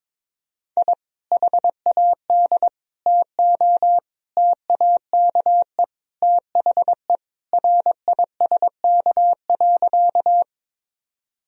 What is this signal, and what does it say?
Morse code, heard as a keyed tone, about 22 wpm, 700 Hz. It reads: I HAD TO TAKE THE RISK.
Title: Sentences from Top 1000 Words - Review Part 4 - 22wpm